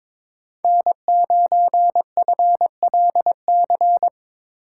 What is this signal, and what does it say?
Morse code, heard as a keyed tone, about 22 wpm, 700 Hz. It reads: N9FLC
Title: USA Call Signs - 5 Characters - 2x Character Spacing - 22wpm